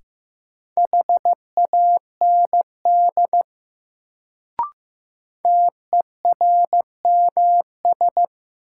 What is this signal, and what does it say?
Morse code, heard as a keyed tone, about 15 wpm, 700 Hz. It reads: HAND TERMS
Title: Sets of 3 Words - New Word Drill - 15wpm